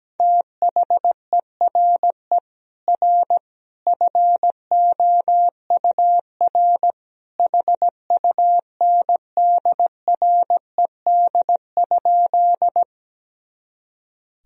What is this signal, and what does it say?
Morse code, heard as a keyed tone, about 17 wpm, 700 Hz. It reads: THERE R FOUR HUNDRED?
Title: Sentences from Top 400 Words and Abbreviations 17wpm